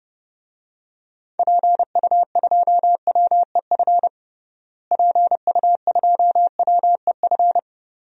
Morse code, heard as a keyed tone, about 32 wpm, 700 Hz. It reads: PU2WEF PU2WEF
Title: DXCC-100 Call Signs - 6 Characters - Repeated 3x - 30wpm